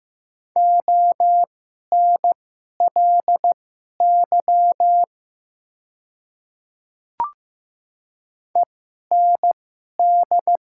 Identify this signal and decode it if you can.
Morse code, heard as a keyed tone, about 15 wpm, spaced as for 11 wpm, 700 Hz. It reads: ONLY END
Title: 2x Character Spacing - Top 200 Words - 15wpm